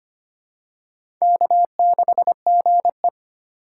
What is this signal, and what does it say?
Morse code, heard as a keyed tone, about 25 wpm, 700 Hz. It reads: K6GE
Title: USA Call Signs - 4 Characters - 2x Character Spacing - 25wpm